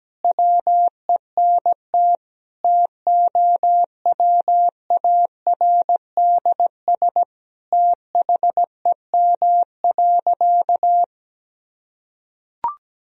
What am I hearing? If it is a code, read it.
Morse code, heard as a keyed tone, about 17 wpm, 700 Hz. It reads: WENT TOWARDS THEM.
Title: Sentences from Top 1000 Words - Review Part 2 - 17wpm